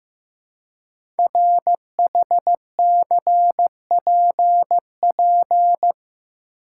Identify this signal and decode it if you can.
Morse code, heard as a keyed tone, about 15 wpm, 700 Hz. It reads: RHCPP
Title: Code Groups - 5 Letters - 15wpm